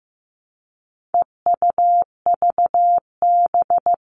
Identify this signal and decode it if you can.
Morse code, heard as a keyed tone, about 15 wpm, 700 Hz. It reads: EUVB
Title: All Cut Numbers - 4-Digit 15wpm